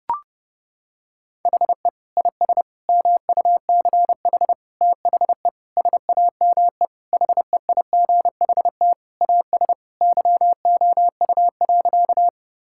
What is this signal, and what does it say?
Morse code, heard as a keyed tone, about 32 wpm, 700 Hz. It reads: HE IS MUCH THE SAME HEIGHT AS YOU.
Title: Sentences from Top 1600 Words 30wpm